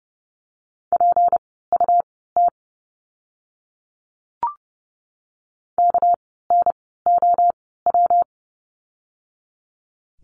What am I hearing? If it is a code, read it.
Morse code, heard as a keyed tone, about 30 wpm, spaced as for 17 wpm, 700 Hz. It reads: PUT KNOW
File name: course / 30:10wpm / 237 Lesson 38f - Letter q - Words 30wpm